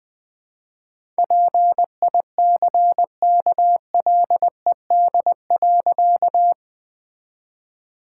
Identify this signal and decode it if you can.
Morse code, heard as a keyed tone, about 20 wpm, 700 Hz. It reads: PICKLED.
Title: All-Age Appropriate Jokes - 20wpm